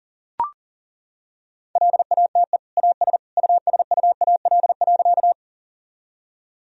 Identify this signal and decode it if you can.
Morse code, heard as a keyed tone, about 40 wpm, 700 Hz. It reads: LATE AS USUAL.
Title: Sentences from Top 1000 Words - Review Part 2 - 40wpm